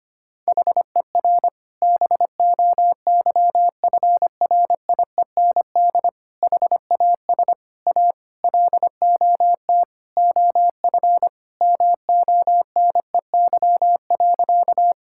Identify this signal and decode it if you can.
Morse code, heard as a keyed tone, about 25 wpm, 700 Hz. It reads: HER BOYFRIEND HAS A LOT OF MONEY.
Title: Sentences from Top 1000 Words - Review Part 5 - 25wpm